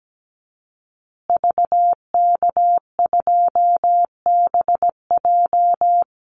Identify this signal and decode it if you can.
Morse code, heard as a keyed tone, about 17 wpm, 700 Hz. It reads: VK2BJ
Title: Call Signs Repeated 2x - 17wpm